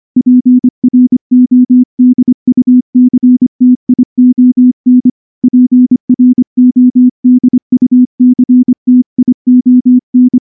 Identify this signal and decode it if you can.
Morse code, heard as a keyed tone, about 24 wpm, 250 Hz. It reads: PRODUCTION PRODUCTION
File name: morse / morse-training-25WPM